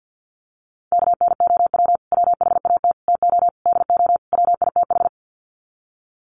Morse code, heard as a keyed tone, about 50 wpm, 700 Hz. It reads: KNOW WHAT TO DO WITH
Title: Sets of 5 Words 50wpm